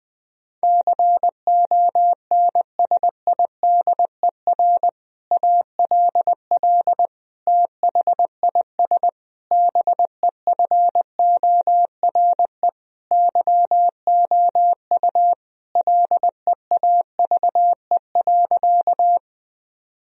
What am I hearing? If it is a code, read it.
Morse code, heard as a keyed tone, about 20 wpm, 700 Hz. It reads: CONSIDER ALL THIS BEFORE YOU LEAVE.
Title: Sentences from Top 1000 Words - Review Part 4 - 20wpm